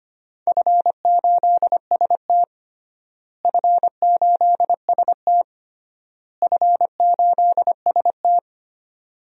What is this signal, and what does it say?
Morse code, heard as a keyed tone, about 25 wpm, 700 Hz. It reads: F8ST F8ST F8ST
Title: DXCC-100 Call Signs - 4 Characters - Repeated 3x - 25wpm